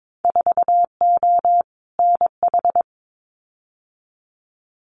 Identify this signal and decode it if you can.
Morse code, heard as a keyed tone, about 22 wpm, 700 Hz. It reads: 4O NH
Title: Winter Field Day Exchanges Encore 22wpm